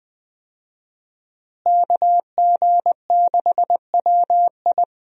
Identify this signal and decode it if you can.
Morse code, heard as a keyed tone, about 20 wpm, 700 Hz. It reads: KG6WI
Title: USA Call Signs - 5 Characters - Repeated 3x - 20wpm